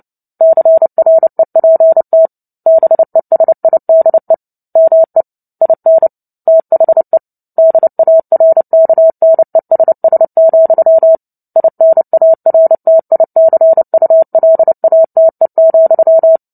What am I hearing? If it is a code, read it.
Morse code, heard as a keyed tone, about 30 wpm, 650 Hz. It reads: CREPT BESIDE ME IN THE DARKNESS, INARTICULATE,
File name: Book Two / 03-Chapter 3 - 29 WPM